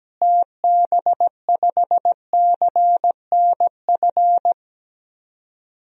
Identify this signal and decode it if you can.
Morse code, heard as a keyed tone, about 17 wpm, 700 Hz. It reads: TB5CNF
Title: DXCC-100 Call Signs - 6 Characters - Repeated 2x - 17wpm